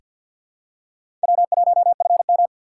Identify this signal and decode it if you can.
Morse code, heard as a keyed tone, about 50 wpm, 700 Hz. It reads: W1FM
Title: USA Call Signs - 4 Characters - 6x Character Spacing - 50wpm